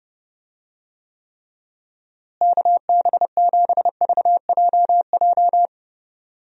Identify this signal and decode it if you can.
Morse code, heard as a keyed tone, about 30 wpm, 700 Hz. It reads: KB7VJJ
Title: USA Call Signs - 6 Characters - Repeated 3x - 30wpm